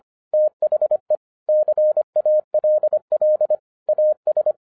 Morse code, heard as a keyed tone, about 25 wpm, 610 Hz. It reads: THE CALL AS
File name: 2012-01-mp3 / n06-FIRST FISTS CONTACT-25wpm